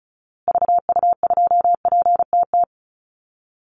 Morse code, heard as a keyed tone, about 34 wpm, 700 Hz. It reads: VU2PTT
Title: Call Signs Repeated 2x - 35wpm